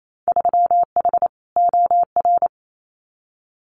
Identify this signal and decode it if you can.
Morse code, heard as a keyed tone, about 28 wpm, 700 Hz. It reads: 3H OR
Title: Winter Field Day Exchanges Encore 28wpm